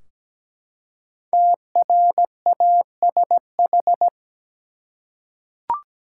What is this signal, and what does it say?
Morse code, heard as a keyed tone, about 17 wpm, 700 Hz. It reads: TRASH
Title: Five Letter Words - 17wpm